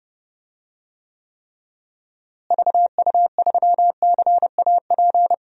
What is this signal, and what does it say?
Morse code, heard as a keyed tone, about 30 wpm, 700 Hz. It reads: VU3CAP
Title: DXCC-100 Call Signs - 6 Characters - Repeated 3x - 30wpm